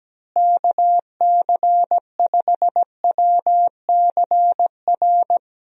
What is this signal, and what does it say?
Morse code, heard as a keyed tone, about 17 wpm, 700 Hz. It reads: KC5WCR
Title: USA Call Signs - 6 Characters - Repeated 2x - 17wpm